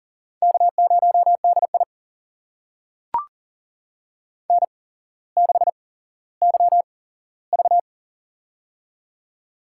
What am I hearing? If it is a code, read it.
Morse code, heard as a keyed tone, about 40 wpm, spaced as for 17 wpm, 700 Hz. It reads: K0DI N6YV
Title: USA Call Signs - 4 Characters - 8x Character Spacing - 40wpm